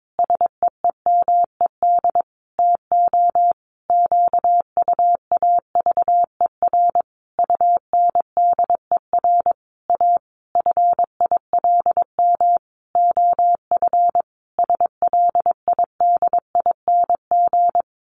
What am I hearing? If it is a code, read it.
Morse code, heard as a keyed tone, about 22 wpm, 700 Hz. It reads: SEEMED TO QUAVER UNDER A FILM OF SLIDING
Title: A Sound Of Thunder 22wpm